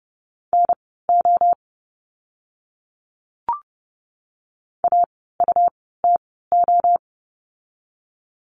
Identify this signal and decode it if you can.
Morse code, heard as a keyed tone, about 30 wpm, spaced as for 17 wpm, 700 Hz. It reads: NO AUTO
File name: Lesson 17f - Letter u - Words 30wpm